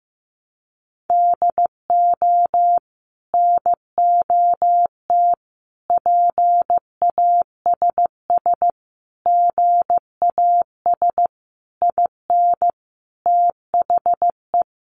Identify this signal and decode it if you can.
Morse code, heard as a keyed tone, about 15 wpm, 700 Hz. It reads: DO NOT PASS GAS IN THE
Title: Sentences from Top 1100 Words 15wpm